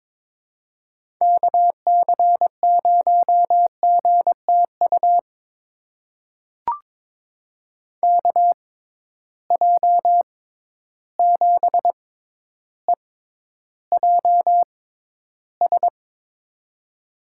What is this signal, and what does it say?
Morse code, heard as a keyed tone, about 22 wpm, spaced as for 11 wpm, 700 Hz. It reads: KC0GTU KJ7EJS
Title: USA Call Signs - 6 Characters - 6x Character Spacing - 22wpm